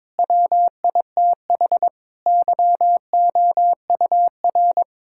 Morse code, heard as a keyed tone, about 22 wpm, 700 Hz. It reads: WITH YOUR
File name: Difficult Sentences with Hints - 22wpm